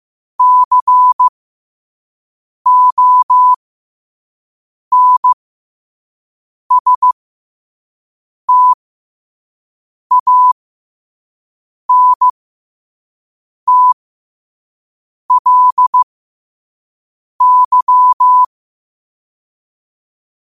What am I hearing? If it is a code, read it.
Morse code, heard as a keyed tone, about 15 wpm, spaced as for 5 wpm, 1000 Hz. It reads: CONSTANTLY
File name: qotdfiles / 5wpm